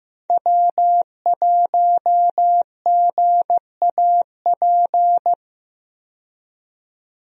Morse code, heard as a keyed tone, about 15 wpm, 700 Hz. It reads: W1GAP
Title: USA Call Signs - 5 Characters - 15wpm